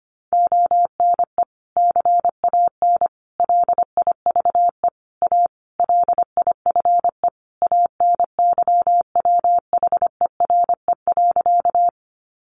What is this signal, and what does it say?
Morse code, heard as a keyed tone, about 25 wpm, 700 Hz. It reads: ONE CAN LIVE A LIFE ANYWHERE.
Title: Sentences from Top 1400 Words 25wpm